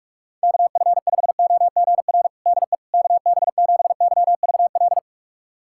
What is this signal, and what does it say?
Morse code, heard as a keyed tone, about 41 wpm, 700 Hz. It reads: KF5OGR DE KB7YVL
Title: USA Callsign DE Callsign - 6 Characters - 45wpm